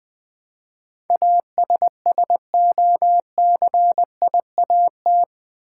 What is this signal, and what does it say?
Morse code, heard as a keyed tone, about 20 wpm, 700 Hz. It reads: ASSOCIAT
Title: Build A Word - 4 Letters At A Time - 11-Letter Words - 20wpm